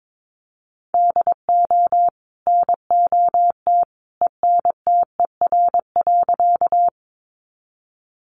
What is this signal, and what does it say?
Morse code, heard as a keyed tone, about 22 wpm, 700 Hz. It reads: DO NOT ENTER.
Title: Sentences from Top 1300 Words 22wpm